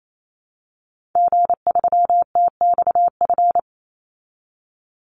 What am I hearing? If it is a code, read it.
Morse code, heard as a keyed tone, about 28 wpm, 700 Hz. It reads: G3TXF
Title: Call Signs Repeated 3x - 28wpm